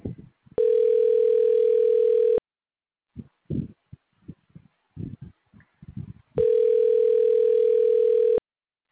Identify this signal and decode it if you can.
Ringback tone (North America), 440 + 480 Hz.